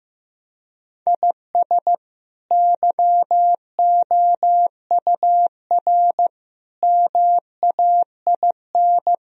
Morse code, heard as a keyed tone, about 15 wpm, 700 Hz. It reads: IS YOUR MAIN
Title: Sentences from Top 1000 Words - Review Part 2 - 15wpm